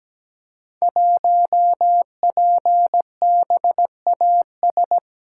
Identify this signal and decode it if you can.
Morse code, heard as a keyed tone, about 17 wpm, 700 Hz. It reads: 1PBAS
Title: Code Groups - 5 Letter:Numbers - 17wpm